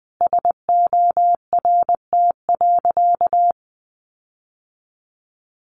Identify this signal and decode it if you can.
Morse code, heard as a keyed tone, about 20 wpm, 700 Hz. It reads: SORT.
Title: Sentences from Top 1300 Words 20wpm